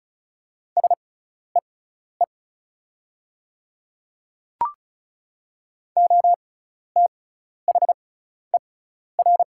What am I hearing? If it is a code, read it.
Morse code, heard as a keyed tone, about 34 wpm, spaced as for 12 wpm, 700 Hz. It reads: SEE OTHER
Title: 6x Character Spacing - Top 200 Words - 35wpm